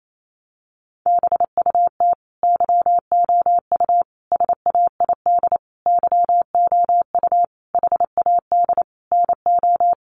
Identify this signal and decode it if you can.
Morse code, heard as a keyed tone, about 28 wpm, 700 Hz. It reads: BUT YOU SAID YOU HAD NO
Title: Sentences from Top 1400 Words 28wpm